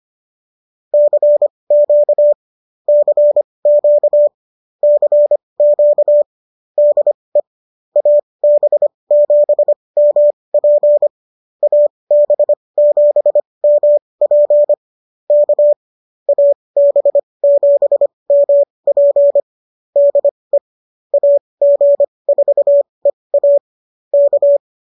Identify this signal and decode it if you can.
Morse code, heard as a keyed tone, about 25 wpm, spaced as for 20 wpm, 590 Hz. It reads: CQ CQ CQ DE AB7MP AB7MP K AB7MP DE AG4EA K